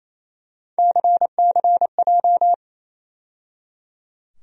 Morse code, heard as a keyed tone, about 28 wpm, 700 Hz. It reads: CCJ
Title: USA Call Sign Postfixes - 3 Letters 28wpm